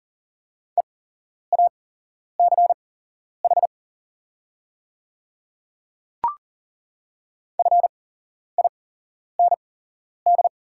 Morse code, heard as a keyed tone, about 40 wpm, spaced as for 11 wpm, 700 Hz. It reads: EACH FIND
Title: 8x Character Spacing - Top 100 Words - 40wpm